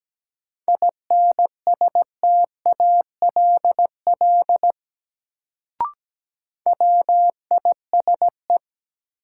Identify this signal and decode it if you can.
Morse code, heard as a keyed tone, about 17 wpm, 700 Hz. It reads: INSTALL WISE
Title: Sentences from Top 1600 Words - New Word Drill - 17wpm